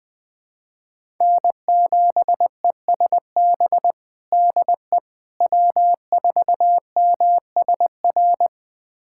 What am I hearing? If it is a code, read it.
Morse code, heard as a keyed tone, about 20 wpm, 700 Hz. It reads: N7ESB DE W4MSR
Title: USA Callsign DE Callsign - 5 Characters - 20wpm